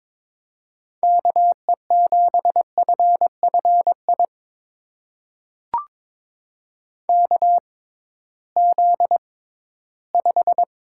Morse code, heard as a keyed tone, about 22 wpm, spaced as for 14 wpm, 700 Hz. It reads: KE7FFI KZ5
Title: USA Call Signs - 6x Character Spacing - 22wpm